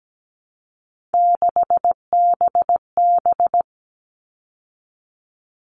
Morse code, heard as a keyed tone, about 17 wpm, 700 Hz. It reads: 6BB
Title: All Cut Numbers - 3-Digit 17wpm